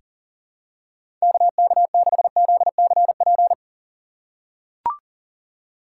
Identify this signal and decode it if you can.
Morse code, heard as a keyed tone, about 40 wpm, 700 Hz. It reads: KK6ZCP
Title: USA Call Signs - 6 Characters - 4x Character Spacing - 40wpm